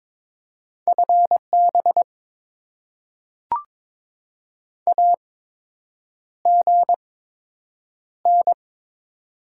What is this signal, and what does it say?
Morse code, heard as a keyed tone, about 22 wpm, spaced as for 8 wpm, 700 Hz. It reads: FB AGN
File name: Top 20 QSO Elements - 8x Character Spacing - 22wpm